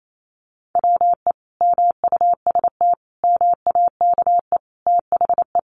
Morse code, heard as a keyed tone, about 28 wpm, 700 Hz. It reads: WE MUST MAKE THE
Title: Sentences from Top 400 Words 28wpm